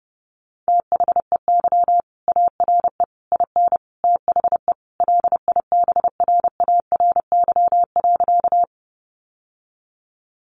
Morse code, heard as a keyed tone, about 28 wpm, 700 Hz. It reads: THEY ARE IN THE LIBRARY.
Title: Sentences from Top 600 Words 30wpm